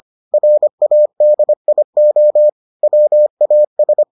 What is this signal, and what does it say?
Morse code, heard as a keyed tone, about 25 wpm, 600 Hz. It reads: RADIO WAS